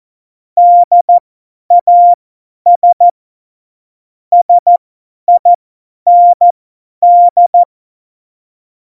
Morse code, heard as a keyed tone, about 14 wpm, spaced as for 10 wpm, 700 Hz. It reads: DAS SIND
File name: Morsekurs / Witze / Witz 03.Juli 2018, 50 F70